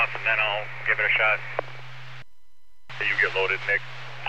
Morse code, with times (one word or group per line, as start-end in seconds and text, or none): none